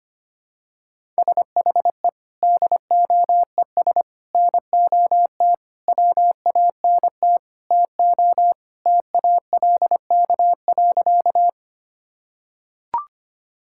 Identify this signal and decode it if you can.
Morse code, heard as a keyed tone, about 25 wpm, 700 Hz. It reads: SHE DOES NOT WANT TO TALK.
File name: practice / 25wpm / Sentences with 4-Word Phrases - 25wpm